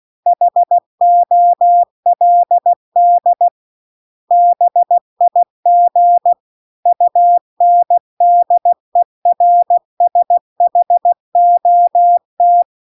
Morse code, heard as a keyed tone, about 16 wpm, 700 Hz. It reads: HOLD BIG UNDERSHOT
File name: CW-The-Big-Twoheaded-River-2-16WPM